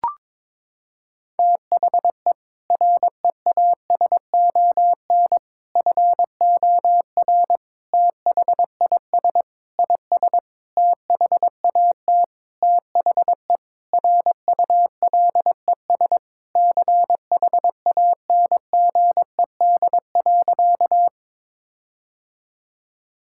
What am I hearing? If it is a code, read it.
Morse code, heard as a keyed tone, about 22 wpm, 700 Hz. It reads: THE REASON FOR THIS IS THAT THE RULES CHANGED.
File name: Sentences with 7-Word Phrases - 22wpm